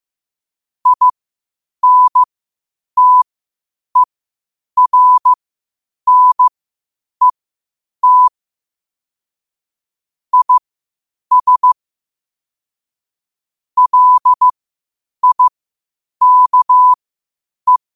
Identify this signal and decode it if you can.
Morse code, heard as a keyed tone, about 15 wpm, spaced as for 8 wpm, 1000 Hz. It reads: INTERNET IS LIKE
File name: qotdfiles / 8wpm